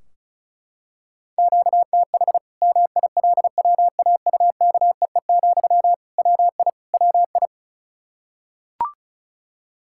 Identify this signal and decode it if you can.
Morse code, heard as a keyed tone, about 34 wpm, 700 Hz. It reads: QTH MILWAUKEE, WI WI
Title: K1USN SST QSO Practice 35wpm